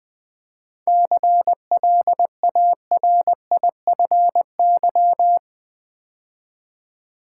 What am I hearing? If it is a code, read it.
Morse code, heard as a keyed tone, about 20 wpm, 700 Hz. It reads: CLARIFY
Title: 7 Letter Words - First 3 Letters Given - 20wpm